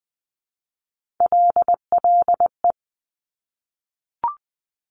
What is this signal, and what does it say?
Morse code, heard as a keyed tone, about 20 wpm, 700 Hz. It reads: LLE
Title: Top Three Letter Combinations in Words 20wpm